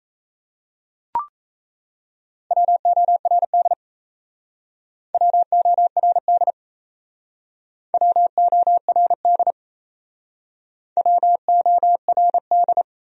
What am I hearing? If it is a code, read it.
Morse code, heard as a keyed tone, about 36 wpm, 700 Hz. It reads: WORD WORD WORD WORD